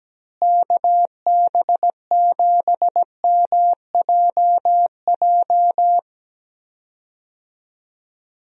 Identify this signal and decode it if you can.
Morse code, heard as a keyed tone, about 17 wpm, 700 Hz. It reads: KB7MJJ
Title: USA Call Signs - 6 Characters - Repeated 3x - 17wpm